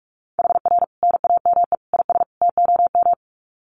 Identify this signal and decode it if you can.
Morse code, heard as a keyed tone, about 44 wpm, 700 Hz. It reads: HR NAME IS TOM
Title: Common QSO Phrases 45wpm